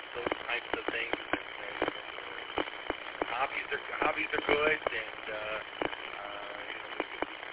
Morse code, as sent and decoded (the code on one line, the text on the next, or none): none